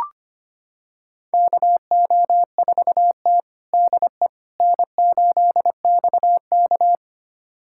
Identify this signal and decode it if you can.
Morse code, heard as a keyed tone, about 25 wpm, 700 Hz. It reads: KO4T DE N8XK
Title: USA Callsign DE Callsign - 4 Characters - 25wpm